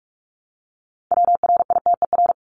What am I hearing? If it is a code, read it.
Morse code, heard as a keyed tone, about 44 wpm, 700 Hz. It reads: WRITER
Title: Top 1751-2000 Words 45wpm